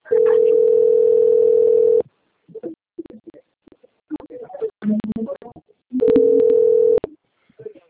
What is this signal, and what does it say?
Ringback tone (North America), 440 + 480 Hz.
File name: out- - - - - .21618